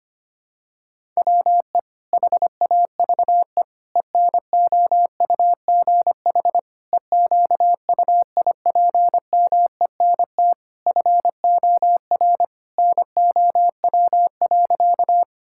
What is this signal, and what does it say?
Morse code, heard as a keyed tone, about 25 wpm, 700 Hz. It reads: WE HAVE ENOUGH EQUIPMENT FOR NOW.
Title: Sentences from Top 1000 Words - Review Part 5 - 25wpm